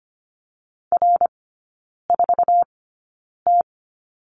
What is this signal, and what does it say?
Morse code, heard as a keyed tone, about 25 wpm, spaced as for 9 wpm, 700 Hz. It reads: R4T
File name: course / 25:10wpm / 093 Lesson 16e - Letter c - Three Characters 25wpm